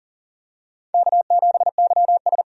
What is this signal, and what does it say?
Morse code, heard as a keyed tone, about 37 wpm, 700 Hz. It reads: K7YS
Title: USA Call Signs - 4 Characters - 40wpm